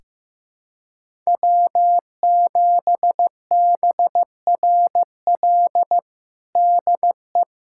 Morse code, heard as a keyed tone, about 15 wpm, 700 Hz. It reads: W7BRL DE
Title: USA Callsign DE Callsign - 5 Characters - 15wpm